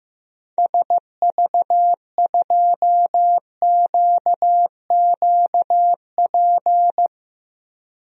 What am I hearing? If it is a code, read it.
Morse code, heard as a keyed tone, about 15 wpm, 700 Hz. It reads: SV2QQP
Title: DXCC-100 Call Signs - 6 Characters - 15wpm